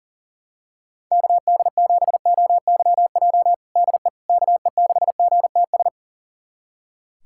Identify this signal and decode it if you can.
Morse code, heard as a keyed tone, about 40 wpm, 700 Hz. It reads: KD7OYJ DE KE6GTS
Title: USA Callsign DE Callsign - 6 Characters - 40wpm